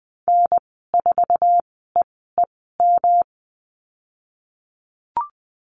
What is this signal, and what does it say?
Morse code, heard as a keyed tone, about 19 wpm, spaced as for 14 wpm, 700 Hz. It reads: N4EEM
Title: 179 Lesson 29g - Slash - Call Signs 20wpm